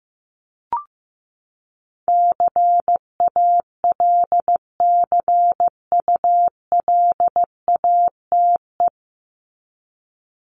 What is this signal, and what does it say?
Morse code, heard as a keyed tone, about 15 wpm, 700 Hz. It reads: CALCULATE
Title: Top 1751-2000 Words 15wpm